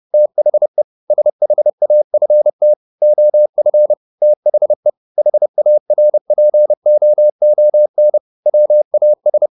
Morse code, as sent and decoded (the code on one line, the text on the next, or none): - .... / ... .... .- ..-. - / --- ..-. / - .... / .... .- .-. .--. --- --- -. / .-- .- ...
THE SHAFT OF THE HARPOON WAS